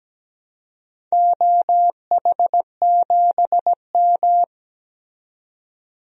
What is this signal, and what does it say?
Morse code, heard as a keyed tone, about 17 wpm, 700 Hz. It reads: OH7M
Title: DXCC-100 Call Signs - 4 Characters - 17wpm